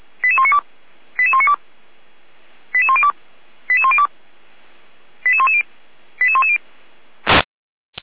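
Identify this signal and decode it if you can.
Five-tone selective call (ZVEI), heard as quick runs of five tones, tones of about 70 ms. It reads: 88192 88192 88102 88102 88110 88110
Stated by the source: ZVEI five-tone selective call received by SDR, from the Feuerwehr Orlamünde fire brigade.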